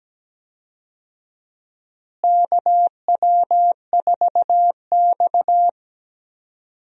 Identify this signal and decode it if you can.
Morse code, heard as a keyed tone, about 17 wpm, 700 Hz. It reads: KW4X
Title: USA Call Signs - 4 Characters - Repeated 2x - 17wpm